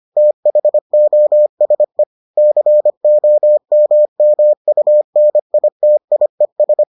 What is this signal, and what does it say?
Morse code, heard as a keyed tone, about 25 wpm, 600 Hz. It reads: THOSE COMMUNITIES